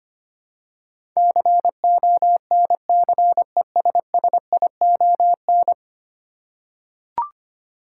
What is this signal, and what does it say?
Morse code, heard as a keyed tone, about 25 wpm, 700 Hz. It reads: CONCESSION
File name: 10 Letter Words - First 3 Letters Given - 25wpm